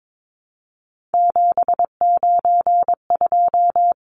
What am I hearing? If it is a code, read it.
Morse code, heard as a keyed tone, about 22 wpm, 700 Hz. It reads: 792
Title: Numbers - 3-Digit 22wpm